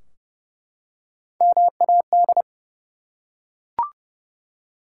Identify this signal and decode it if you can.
Morse code, heard as a keyed tone, about 30 wpm, 700 Hz. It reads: MAD
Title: Three Letter Words - 30wpm